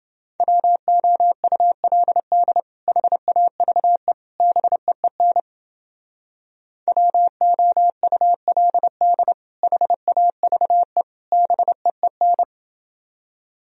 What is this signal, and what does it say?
Morse code, heard as a keyed tone, about 30 wpm, 700 Hz. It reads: WOULD HAVE BEEN WOULD HAVE BEEN